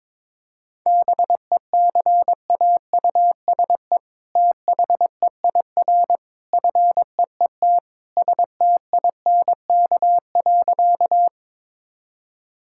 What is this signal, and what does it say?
Morse code, heard as a keyed tone, about 22 wpm, 700 Hz. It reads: BECAUSE THEIR FEET STINK.
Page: morse-code-ninja-practice-22wpm